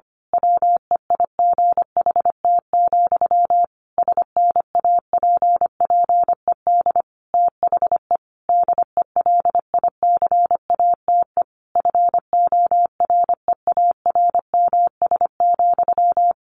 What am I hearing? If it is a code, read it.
Morse code, heard as a keyed tone, about 25 wpm, 700 Hz. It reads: WEIGHT, SNAPPED THE DELICATE FOREARMS,
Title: A Sound Of Thunder 25wpm